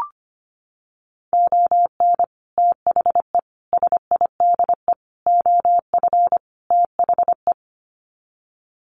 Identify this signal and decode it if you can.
Morse code, heard as a keyed tone, about 25 wpm, 700 Hz. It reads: ON THE SIDE OF THE
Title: Sets of 5 Words 25wpm